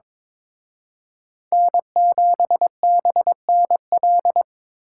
Morse code, heard as a keyed tone, about 26 wpm, 700 Hz. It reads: N7BNL
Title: Speed Racing - USA 1x3 Call Signs 22wpm